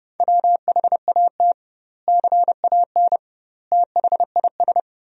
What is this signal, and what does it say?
Morse code, heard as a keyed tone, about 30 wpm, 700 Hz. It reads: WHAT CAN THIS
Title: 2x Word-Spacing - Sentences from Top 100 Words - 30wpm